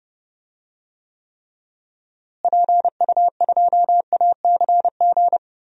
Morse code, heard as a keyed tone, about 30 wpm, 700 Hz. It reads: PU2ACG
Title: DXCC-100 Call Signs - 6 Characters - Repeated 2x - 30wpm